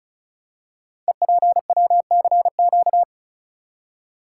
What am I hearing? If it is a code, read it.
Morse code, heard as a keyed tone, about 36 wpm, 690 Hz. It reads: EPWCQ
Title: Code Groups - 5 Letter:Numbers - 35wpm